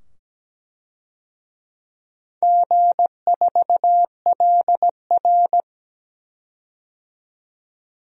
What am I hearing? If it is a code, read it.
Morse code, heard as a keyed tone, about 17 wpm, 700 Hz. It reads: G4LR
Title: DXCC-100 Call Signs - 4 Characters - 17wpm